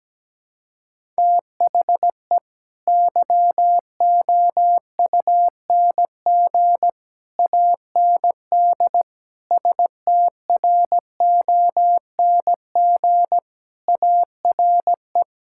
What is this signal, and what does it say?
Morse code, heard as a keyed tone, about 17 wpm, 700 Hz. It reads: THE YOUNG AND STRONG ARE
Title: Sentences from Top 1000 Words - Review Part 5 - 17wpm